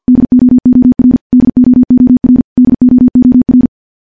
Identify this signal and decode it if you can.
Morse code, heard as a keyed tone, about 50 wpm, 250 Hz. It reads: DOOR DOOR DOOR
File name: morse / morse-training-50WPM